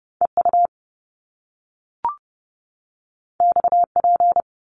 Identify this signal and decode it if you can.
Morse code, heard as a keyed tone, about 30 wpm, 700 Hz. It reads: EU XP
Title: Top Two Letter Combinations in Words 30wpm